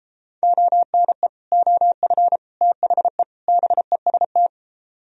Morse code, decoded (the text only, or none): ONE OF THE BEST